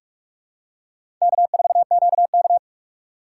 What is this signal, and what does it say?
Morse code, heard as a keyed tone, about 42 wpm, 700 Hz. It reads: K4QK
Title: USA Call Signs - 4 Characters - 4x Character Spacing - 45wpm